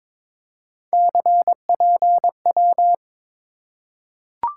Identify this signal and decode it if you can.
Morse code, heard as a keyed tone, about 22 wpm, 700 Hz. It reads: CPW
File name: practice / 22wpm / Three Letter-Number Combinations - 22wpm